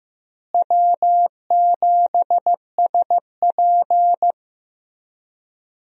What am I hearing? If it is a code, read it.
Morse code, heard as a keyed tone, about 15 wpm, 700 Hz. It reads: W7SP
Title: USA Call Signs - 4 Characters - Repeated 3x - 15wpm